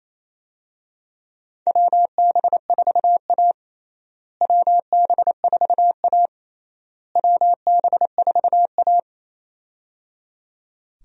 Morse code, heard as a keyed tone, about 28 wpm, 700 Hz. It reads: WB4A WB4A WB4A
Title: USA Call Signs - 4 Characters - Repeated 3x - 28wpm